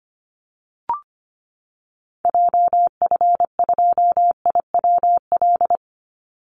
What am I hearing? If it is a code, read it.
Morse code, heard as a keyed tone, about 25 wpm, 700 Hz. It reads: JF2IWL
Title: Call Signs Repeated 2x - 25wpm